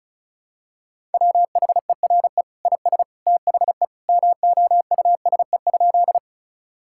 Morse code, heard as a keyed tone, about 36 wpm, 700 Hz. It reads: WHERE IS THE MOUSE?
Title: Sentences from Top 1000 Words - Review Part 5 - 35wpm